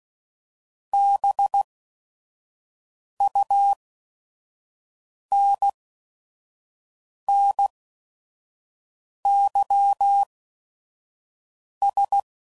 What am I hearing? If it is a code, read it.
Morse code, heard as a keyed tone, about 16 wpm, spaced as for 5 wpm, 800 Hz. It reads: BUNNYS